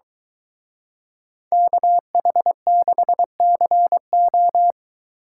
Morse code, heard as a keyed tone, about 22 wpm, 700 Hz. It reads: KH6CO